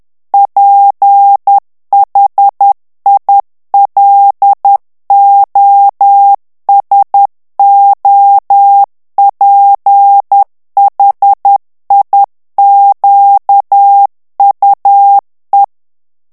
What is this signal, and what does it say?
Morse code, heard as a keyed tone, about 10 wpm, 800 Hz. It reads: PHILOSOPHIQUE
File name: CW / FICHIERS-MP3 / FRANCAIS TEXTE CLAIR 2 VITESSE 10 MOTS MINUTES